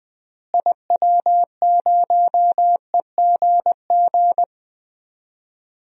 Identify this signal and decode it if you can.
Morse code, heard as a keyed tone, about 20 wpm, 700 Hz. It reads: IW0EGG
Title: DXCC-100 Call Signs - 6 Characters - Repeated 2x - 20wpm